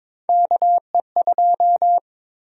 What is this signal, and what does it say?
Morse code, heard as a keyed tone, about 21 wpm, 700 Hz. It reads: KE2
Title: USA Call Sign Prefixes - Repeated 2x - 22wpm